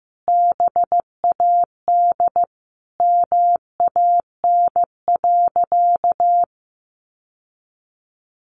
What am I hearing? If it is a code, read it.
Morse code, heard as a keyed tone, about 15 wpm, 700 Hz. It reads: BAD MAN.
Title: Sentences from Top 1200 Words 15wpm